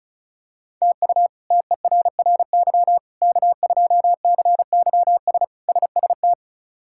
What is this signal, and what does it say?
Morse code, heard as a keyed tone, about 34 wpm, 700 Hz. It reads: TU TERRY K2CYS SST
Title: K1USN SST QSO Practice - 35wpm